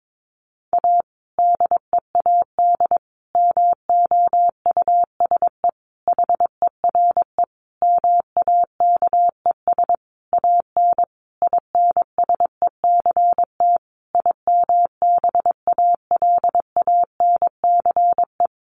Morse code, heard as a keyed tone, about 22 wpm, 700 Hz. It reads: A DEAD MOUSE HERE MAKES AN INSECT IMBALANCE
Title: A Sound Of Thunder 22wpm